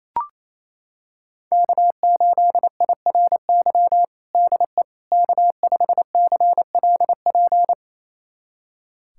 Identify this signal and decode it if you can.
Morse code, heard as a keyed tone, about 28 wpm, 700 Hz. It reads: K8IRY DE K5CLP
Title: USA Callsign DE Callsign - 5 Characters - 28wpm